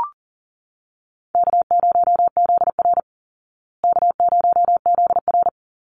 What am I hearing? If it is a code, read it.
Morse code, heard as a keyed tone, about 40 wpm, 700 Hz. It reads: K0ZR K0ZR
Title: Easy Call Signs Repeated 3x - 40wpm